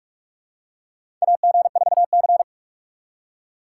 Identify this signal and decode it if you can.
Morse code, heard as a keyed tone, about 46 wpm, 700 Hz. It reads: AG4C
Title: USA Call Signs - 4 Characters - 8x Character Spacing - 45wpm